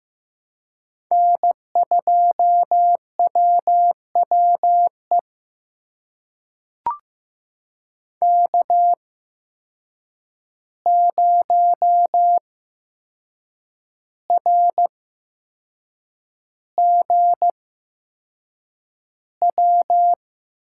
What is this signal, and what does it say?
Morse code, heard as a keyed tone, about 15 wpm, spaced as for 6 wpm, 700 Hz. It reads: N2WWE K0RGW
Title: USA Call Signs - 5 Characters - 8x Character Spacing - 15wpm